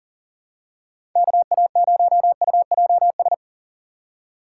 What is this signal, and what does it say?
Morse code, heard as a keyed tone, about 40 wpm, 700 Hz. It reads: KA0UJS
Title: USA Call Signs - 4x Character Spacing - 40wpm